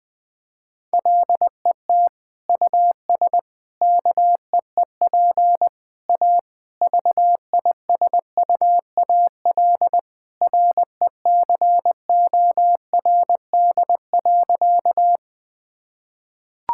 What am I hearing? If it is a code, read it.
Morse code, heard as a keyed tone, about 20 wpm, 700 Hz. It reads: LET US KEEP A VISUAL RECORD.
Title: Sentences from Top 1000 Words - Review Part 6 - 20wpm